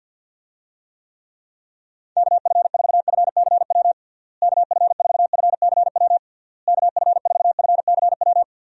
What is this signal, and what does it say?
Morse code, heard as a keyed tone, about 51 wpm, 700 Hz. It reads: KF4FCW KF4FCW KF4FCW
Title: USA Call Signs - 6 Characters - Repeated 3x - 50wpm